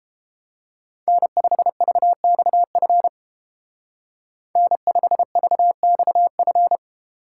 Morse code, heard as a keyed tone, about 31 wpm, 700 Hz. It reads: N5VXF N5VXF